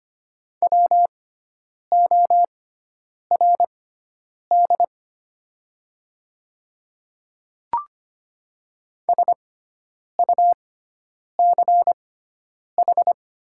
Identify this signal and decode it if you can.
Morse code, heard as a keyed tone, about 25 wpm, spaced as for 9 wpm, 690 Hz. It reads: WORD SUCH
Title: 6x Character Spacing - Top 300 Words - 25wpm